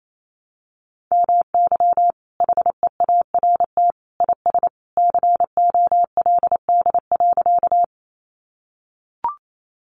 Morse code, heard as a keyed tone, about 28 wpm, 700 Hz. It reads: MY HEART IS COLD.
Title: Sentences from Top 1200 Words 28wpm